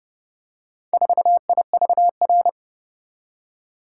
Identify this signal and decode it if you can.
Morse code, heard as a keyed tone, about 29 wpm, 700 Hz. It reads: 4IVR